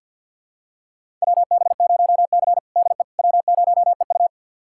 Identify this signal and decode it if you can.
Morse code, heard as a keyed tone, about 47 wpm, 700 Hz. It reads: WB0C DE W0EU
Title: USA Callsign DE Callsign - 4 Characters - 50wpm